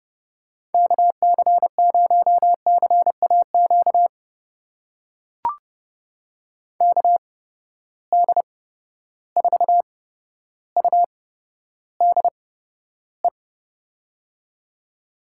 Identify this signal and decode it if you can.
Morse code, heard as a keyed tone, about 32 wpm, spaced as for 13 wpm, 700 Hz. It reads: KC0CAQ KD4UDE
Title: USA Call Signs - 6 Characters - 8x Character Spacing - 30wpm